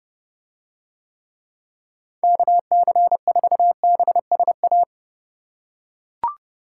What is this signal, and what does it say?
Morse code, heard as a keyed tone, about 30 wpm, 700 Hz. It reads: KC4BSA